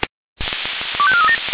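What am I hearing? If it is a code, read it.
Five-tone selective call (ZVEI), heard as quick runs of five tones, tones of about 70 ms. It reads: 25437
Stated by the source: ZVEI five-tone selective call received on SDR.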